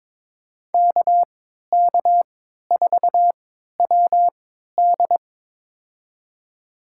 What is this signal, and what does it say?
Morse code, heard as a keyed tone, about 22 wpm, spaced as for 13 wpm, 700 Hz. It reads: KK4WD
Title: USA Call Signs - 5 Characters - 3x Character Spacing - 22wpm